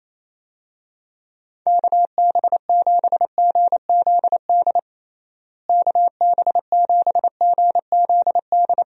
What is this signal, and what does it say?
Morse code, heard as a keyed tone, about 27 wpm, 700 Hz. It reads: KB7GZD KB7GZD
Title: USA Call Signs - 6 Characters - Repeated 2x - 28wpm